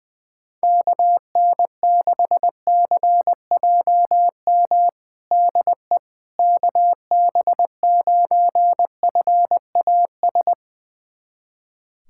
Morse code, heard as a keyed tone, about 20 wpm, 700 Hz. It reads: KN6CJM DE KB9FAS
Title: USA Callsign DE Callsign 20wpm